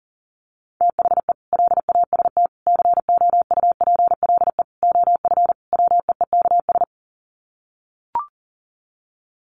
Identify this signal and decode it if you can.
Morse code, heard as a keyed tone, about 40 wpm, 700 Hz. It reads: THE LAST COUPLE OF WEEKS
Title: Sets of 5 Words Encore2 40wpm